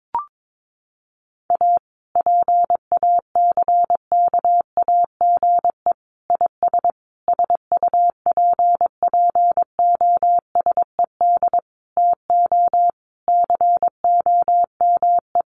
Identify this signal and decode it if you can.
Morse code, heard as a keyed tone, about 22 wpm, 700 Hz. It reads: A PACKAGE IS SUPPOSED TO COME
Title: Easy Phrases 22wpm